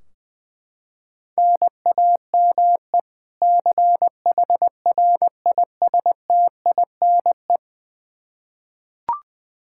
Morse code, heard as a keyed tone, about 21 wpm, 700 Hz. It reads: NAME CHRISTINE
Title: Practice QSO - Exchange 1 - Beginning - 20wpm